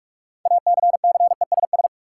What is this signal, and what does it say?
Morse code, heard as a keyed tone, about 46 wpm, 700 Hz. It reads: ACCESS
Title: Sentences from Top 1000 Words - Review Part 4 - New Word Drill - 45wpm